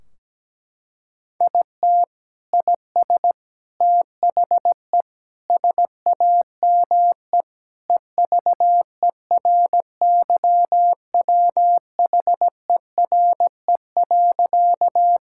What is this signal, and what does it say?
Morse code, heard as a keyed tone, about 17 wpm, 700 Hz. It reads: IT IS THE SAME EVERYWHERE.
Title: Sentences from Top 1600 Words 17wpm